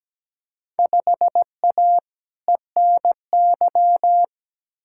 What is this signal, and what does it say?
Morse code, heard as a keyed tone, about 17 wpm, 700 Hz. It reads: 5A ENY
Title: Field Day Exchanges Encore - 17wpm